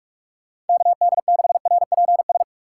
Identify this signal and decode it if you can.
Morse code, heard as a keyed tone, about 46 wpm, 700 Hz. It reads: KD6RPS
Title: USA Call Signs - 6 Characters - 3x Character Spacing - 45wpm